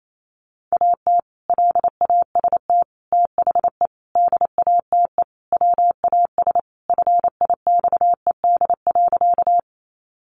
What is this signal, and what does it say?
Morse code, heard as a keyed tone, about 28 wpm, 700 Hz. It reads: AT LAST THE DATE WAS FIXED.
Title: Sentences from Top 1100 Words 28wpm